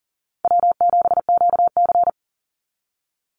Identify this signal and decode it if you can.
Morse code, heard as a keyed tone, about 40 wpm, 700 Hz. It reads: W7QC
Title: Easy Call Signs - 40wpm